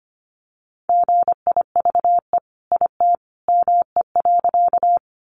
Morse code, heard as a keyed tone, about 25 wpm, 700 Hz. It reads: GIVE IT ME.
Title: Sentences from Top 200 Words 25wpm